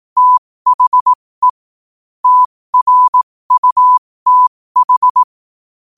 Morse code, heard as a keyed tone, about 18 wpm, spaced as for 15 wpm, 1000 Hz. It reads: THE TRUTH
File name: qotdfiles / 15wpm